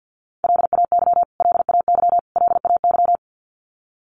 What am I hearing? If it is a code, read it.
Morse code, heard as a keyed tone, about 48 wpm, 700 Hz. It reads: LAY LAY LAY
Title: Top 500 Words Repeated 50wpm